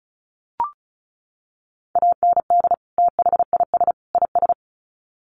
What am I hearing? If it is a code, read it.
Morse code, heard as a keyed tone, about 34 wpm, 700 Hz. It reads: AND THIS IS
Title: Sets of 3 Words 35wpm